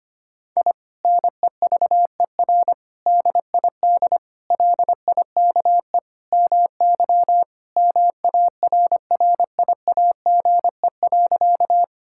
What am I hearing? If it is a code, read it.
Morse code, heard as a keyed tone, about 25 wpm, 700 Hz. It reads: I NEVER DID LIKE MY MARRIAGE.
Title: Sentences from Top 1000 Words - Review Part 5 - 25wpm